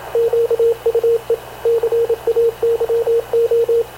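Morse code, heard as a keyed tone, about 27 wpm, 460 Hz, with radio noise. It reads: QUE CAYO
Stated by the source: Havana Radio / CLA sending Spanish press (PX) in Morse, recorded by Radio Officer Paul Dumesnil onboard CNAV QUEST/CZDO.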